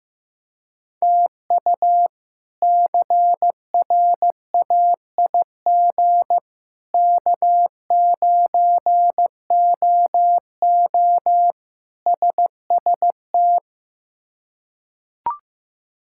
Morse code, heard as a keyed tone, about 16 wpm, 700 Hz. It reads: TU CRAIG K9OO SST
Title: K1USN SST QSO Practice - 15wpm